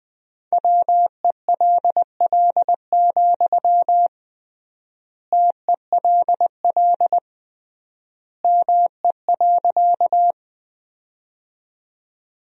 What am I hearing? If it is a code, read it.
Morse code, heard as a keyed tone, about 20 wpm, 700 Hz. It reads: WELL, TELL ME.
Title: 3x Word-Spacing - Sentences from Top 200 Words - 20wpm